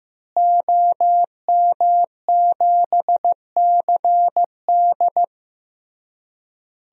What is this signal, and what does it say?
Morse code, heard as a keyed tone, about 15 wpm, 700 Hz. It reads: OM7CD
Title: DXCC-100 Call Signs - 5 Characters - Repeated 3x - 15wpm